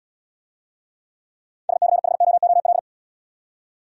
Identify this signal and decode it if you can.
Morse code, heard as a keyed tone, about 75 wpm, 700 Hz. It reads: SCHOOL